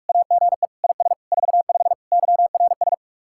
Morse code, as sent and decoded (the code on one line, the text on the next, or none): .- --. / .. ... / ....- ..... / -.-- .-. ...
AGE IS 45 YRS